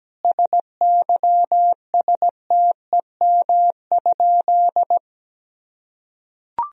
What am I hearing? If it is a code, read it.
Morse code, heard as a keyed tone, about 17 wpm, 700 Hz. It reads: SYSTEM?
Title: Sentences from Top 1000 Words - Review Part 3 - 17wpm